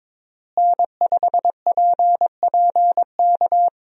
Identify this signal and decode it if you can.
Morse code, heard as a keyed tone, about 22 wpm, 700 Hz. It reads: N5PPK
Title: USA Call Signs - 5 Characters - Repeated 3x - 22wpm